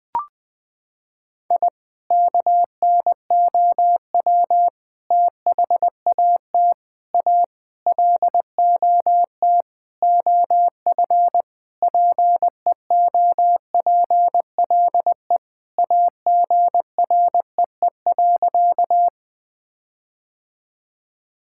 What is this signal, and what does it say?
Morse code, heard as a keyed tone, about 20 wpm, 700 Hz. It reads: I KNOW THAT A LOT OF PEOPLE AGREE.
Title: Sentences with 5-Word Phrases - 20wpm